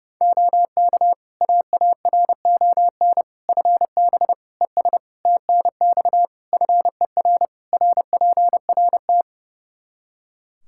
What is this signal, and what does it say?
Morse code, heard as a keyed tone, about 30 wpm, 700 Hz. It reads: OK AARON FB ES TNX FER RPRT
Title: Practice QSO - Exchange 2 - Info - 30wpm